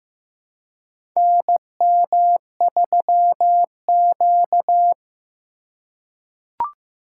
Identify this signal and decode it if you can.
Morse code, heard as a keyed tone, about 15 wpm, 690 Hz. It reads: NM3Q
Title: USA Call Signs - 4 Characters - 2x Character Spacing - 15wpm